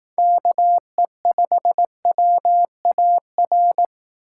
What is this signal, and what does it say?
Morse code, heard as a keyed tone, about 18 wpm, 700 Hz. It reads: KE5WAR
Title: Speed Racing - USA 2x3 Call Signs 15wpm